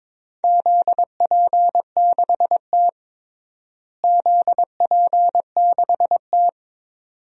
Morse code, heard as a keyed tone, about 22 wpm, 700 Hz. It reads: ZP6T ZP6T
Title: DXCC-100 Call Signs - 4 Characters - Repeated 2x - 22wpm